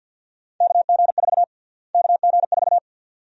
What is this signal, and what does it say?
Morse code, heard as a keyed tone, about 51 wpm, 700 Hz. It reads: KG4 KG4
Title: USA Call Sign Prefixes - Repeated 2x - 50wpm